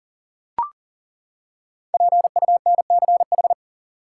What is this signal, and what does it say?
Morse code, heard as a keyed tone, about 40 wpm, 700 Hz. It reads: PUNCH
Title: Five Letter Words - 40wpm